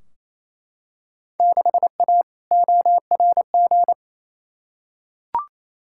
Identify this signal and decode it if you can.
Morse code, heard as a keyed tone, about 28 wpm, 700 Hz. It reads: 6A ORG
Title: Field Day Exchanges Encore - 28wpm